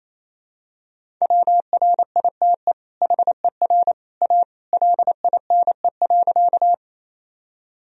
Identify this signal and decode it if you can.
Morse code, heard as a keyed tone, about 27 wpm, 700 Hz. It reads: WRITE HER A LINE.
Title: Sentences from Top 1000 Words - Review Part 1 - 28wpm